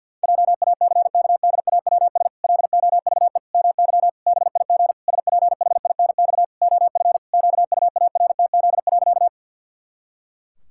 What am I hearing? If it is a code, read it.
Morse code, heard as a keyed tone, about 48 wpm, 700 Hz. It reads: JACKDAWS LOVE MY BIG SPHINX OF QUARTZ.
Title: Pangram Challenge v2 - 50wpm